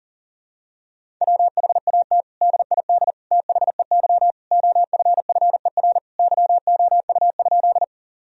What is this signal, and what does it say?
Morse code, heard as a keyed tone, about 37 wpm, 700 Hz. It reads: WHAT DID THEY OFFER YOU?
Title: Sentences from Top 1000 Words - Review Part 4 - 40wpm